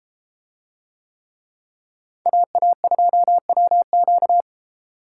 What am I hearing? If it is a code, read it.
Morse code, heard as a keyed tone, about 32 wpm, 700 Hz. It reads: AA2WQ